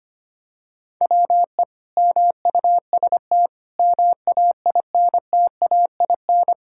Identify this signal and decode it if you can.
Morse code, heard as a keyed tone, about 25 wpm, 700 Hz. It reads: WE MUST MAINTAIN
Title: Sentences from Top 1000 Words - Review Part 5 - 25wpm